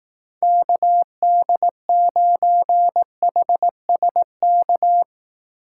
Morse code, heard as a keyed tone, about 18 wpm, 700 Hz. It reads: KD9HSK
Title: Speed Racing - USA 2x3 Call Signs 15wpm